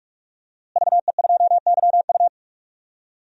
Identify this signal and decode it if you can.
Morse code, heard as a keyed tone, about 46 wpm, 700 Hz. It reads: VE2YU
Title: DXCC-100 Call Signs - 5 Characters - Repeated 2x - 45wpm